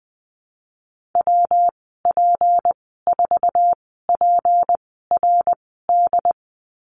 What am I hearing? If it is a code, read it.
Morse code, heard as a keyed tone, about 20 wpm, spaced as for 14 wpm, 700 Hz. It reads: WP4PRD
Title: 179 Lesson 29g - Slash - Call Signs 20wpm